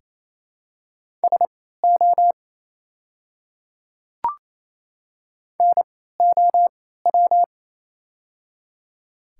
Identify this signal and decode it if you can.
Morse code, heard as a keyed tone, about 28 wpm, spaced as for 16 wpm, 700 Hz. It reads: SO NOW
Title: 3x Character Spacing - Top 100 Words - 28wpm